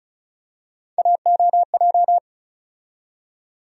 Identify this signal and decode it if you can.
Morse code, heard as a keyed tone, about 34 wpm, 700 Hz. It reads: AOJ
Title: USA Call Sign Postfixes - 3 Letters 35wpm